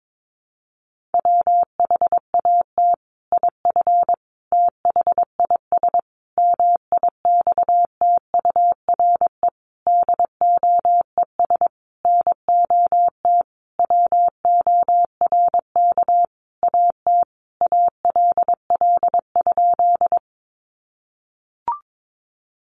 Morse code, heard as a keyed tone, about 22 wpm, 700 Hz. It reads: WHAT IF THIS MIXTURE DOES NOT WORK AT ALL?
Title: Sentences from Top 1500 Words 22wpm